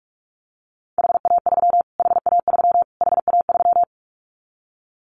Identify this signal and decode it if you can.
Morse code, heard as a keyed tone, about 44 wpm, 700 Hz. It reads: HA3 HA3 HA3
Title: Callsign Prefixes for Top 25 DXCC - Repeated 3x - 45wpm